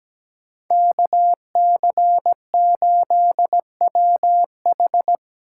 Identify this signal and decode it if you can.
Morse code, heard as a keyed tone, about 17 wpm, 700 Hz. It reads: KC8WH
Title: K1USN SST QSO Practice - 17wpm